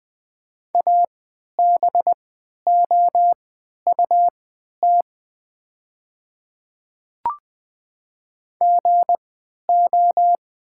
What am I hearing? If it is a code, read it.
Morse code, heard as a keyed tone, about 20 wpm, spaced as for 11 wpm, 700 Hz. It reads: ABOUT GO
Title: 3x Character Spacing - Top 100 Words - 20wpm